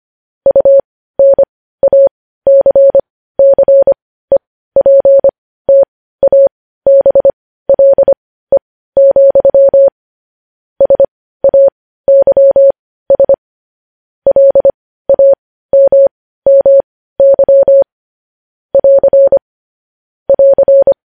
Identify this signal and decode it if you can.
Morse code, heard as a keyed tone, about 25 wpm, spaced as for 15 wpm, 550 Hz. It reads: UNACCEPTABLE, SAYS LAMMY <AR> <AR>
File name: podcast / News headlines-1x-25x15